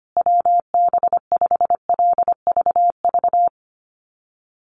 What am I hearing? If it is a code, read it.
Morse code, heard as a keyed tone, about 25 wpm, 700 Hz. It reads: WB5LVV
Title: USA Call Signs Repeated 2x 25wpm